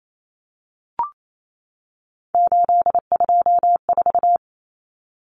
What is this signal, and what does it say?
Morse code, heard as a keyed tone, about 28 wpm, 700 Hz. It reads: 824
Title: Common Cut Numbers - 3-Digit 28wpm